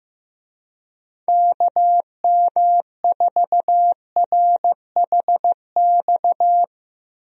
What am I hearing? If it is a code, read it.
Morse code, heard as a keyed tone, about 15 wpm, 700 Hz. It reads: KM4RHX
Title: USA Call Signs - 6 Characters - Repeated 3x - 15wpm